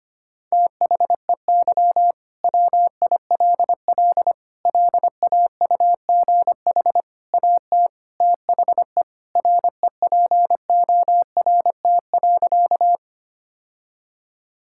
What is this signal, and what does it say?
Morse code, heard as a keyed tone, about 25 wpm, 700 Hz. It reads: THEY WILL LAUGH AT THE REPORT.
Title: Sentences from Top 1000 Words - Review Part 6 - 25wpm